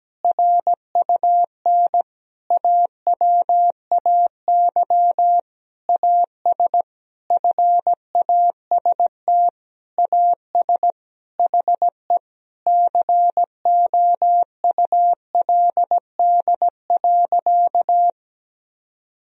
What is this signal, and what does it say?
Morse code, heard as a keyed tone, about 17 wpm, 700 Hz. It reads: RUN AWAY AS FAST AS HE COULD.
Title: Sentences from Top 1000 Words - Review Part 2 - 17wpm